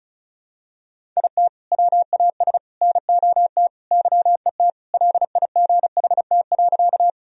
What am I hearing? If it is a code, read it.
Morse code, heard as a keyed tone, about 36 wpm, 700 Hz. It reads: IT WAS NOT YET LIGHT.
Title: Sentences from Top 1000 Words - Review Part 1 - 35wpm